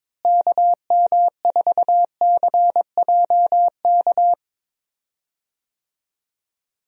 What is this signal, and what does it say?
Morse code, heard as a keyed tone, about 22 wpm, 700 Hz. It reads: KM4CJK
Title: USA Call Signs - 6 Characters - Repeated 3x - 22wpm